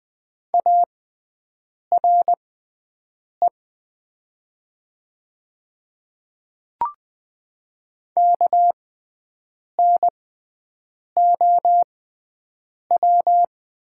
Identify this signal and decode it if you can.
Morse code, heard as a keyed tone, about 20 wpm, spaced as for 7 wpm, 700 Hz. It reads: ARE KNOW
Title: 6x Character Spacing - Top 100 Words - 20wpm